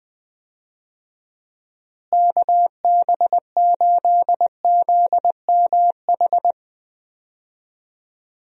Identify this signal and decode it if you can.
Morse code, heard as a keyed tone, about 20 wpm, 700 Hz. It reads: KB8ZMH
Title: USA Call Signs - 6 Characters - Repeated 3x - 20wpm